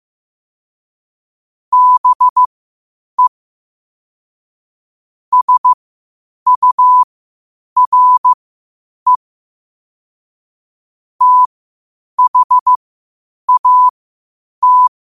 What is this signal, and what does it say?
Morse code, heard as a keyed tone, about 15 wpm, spaced as for 8 wpm, 1000 Hz. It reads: BE SURE THAT
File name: qotdfiles / 8wpm